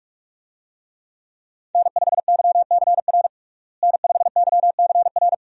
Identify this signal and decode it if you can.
Morse code, heard as a keyed tone, about 45 wpm, 700 Hz. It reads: N5YCR N5YCR
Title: USA Call Signs - 5 Characters - Repeated 2x - 45wpm